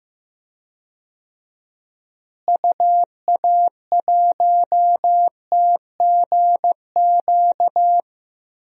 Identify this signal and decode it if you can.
Morse code, heard as a keyed tone, about 15 wpm, 700 Hz. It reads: UA1TGQ
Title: DXCC-100 Call Signs - 6 Characters - Repeated 2x - 15wpm